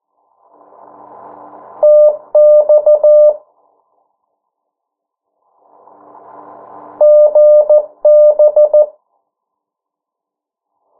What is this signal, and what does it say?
Morse code, heard as a keyed tone, about 14 wpm, 600 Hz. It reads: TX GB
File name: morse mp3 / 09 urok 8